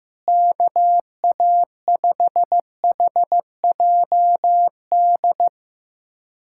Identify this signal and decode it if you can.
Morse code, heard as a keyed tone, about 16 wpm, 700 Hz. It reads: KA5HJD
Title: USA Callsign DE Callsign - 6 Characters - 15wpm